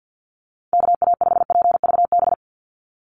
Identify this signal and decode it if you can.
Morse code, heard as a keyed tone, about 49 wpm, 700 Hz. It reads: KA5PVB
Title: NA SOTA Chaser Callsigns 50wpm